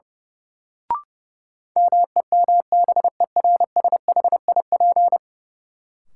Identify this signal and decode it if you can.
Morse code, heard as a keyed tone, about 30 wpm, 700 Hz. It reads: MEMBERSHIP
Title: Sentences from Top 1400 Words - New Word Drill Alt - 30wpm